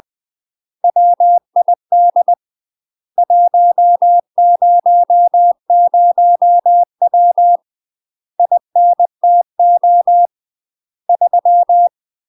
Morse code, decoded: WID 100W INTO 3

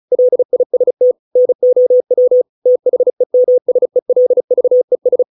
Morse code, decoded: LIST NOW THEMSELVES

